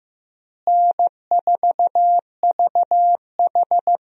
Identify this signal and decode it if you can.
Morse code, heard as a keyed tone, about 15 wpm, 700 Hz. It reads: N4VH
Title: USA Call Signs - 4 Characters - Repeated 3x - 15wpm